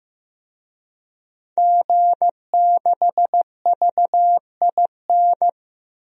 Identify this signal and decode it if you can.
Morse code, heard as a keyed tone, about 16 wpm, 700 Hz. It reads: G6VIN
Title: DXCC-100 Call Signs - 5 Characters - 15wpm